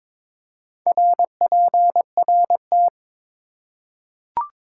Morse code, heard as a keyed tone, about 22 wpm, 700 Hz. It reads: RPRT
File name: Top 90 QSO Elements - 4x Character Spacing - 22wpm